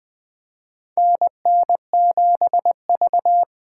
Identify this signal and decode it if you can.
Morse code, heard as a keyed tone, about 20 wpm, 700 Hz. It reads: NN7V
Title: USA Call Signs - 4 Characters - 3x Character Spacing - 20wpm